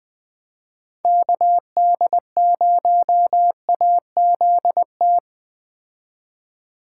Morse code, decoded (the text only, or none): KD0AZT